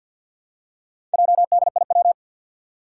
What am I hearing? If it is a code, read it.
Morse code, heard as a keyed tone, about 51 wpm, 700 Hz. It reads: JDIW
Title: Four Challenging Letter-Number Combinations 50wpm